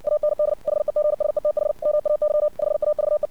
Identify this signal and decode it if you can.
Morse code, heard as a keyed tone, about 64 wpm, 600 Hz. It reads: AND HEGSETH MAY HAVE
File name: archived / CRSnet